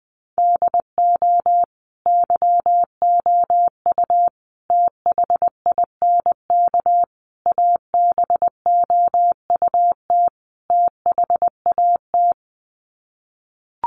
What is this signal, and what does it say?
Morse code, heard as a keyed tone, about 20 wpm, 700 Hz. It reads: DO YOU THINK ABOUT THAT
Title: Sets of 5 Words Encore2 20wpm